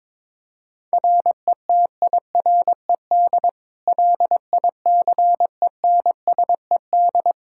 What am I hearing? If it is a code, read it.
Morse code, heard as a keyed tone, about 22 wpm, 700 Hz. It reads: RETIRED LICENSED
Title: Practice QSO - Exchange 3 - More Info - 22wpm